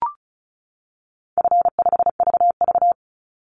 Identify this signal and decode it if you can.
Morse code, heard as a keyed tone, about 34 wpm, 700 Hz. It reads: F5VV
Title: Call Signs - 35wpm